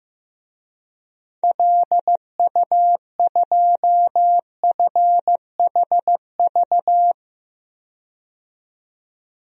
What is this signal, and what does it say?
Morse code, heard as a keyed tone, about 15 wpm, 700 Hz. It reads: LU2FHV
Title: DXCC-100 Call Signs - 6 Characters - 15wpm